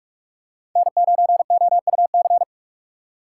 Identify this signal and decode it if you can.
Morse code, heard as a keyed tone, about 46 wpm, 690 Hz. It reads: N9OUC
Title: USA Call Signs - 3x Character Spacing - 45wpm